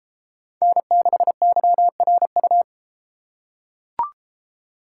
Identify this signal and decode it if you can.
Morse code, heard as a keyed tone, about 33 wpm, 700 Hz. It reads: N6YRU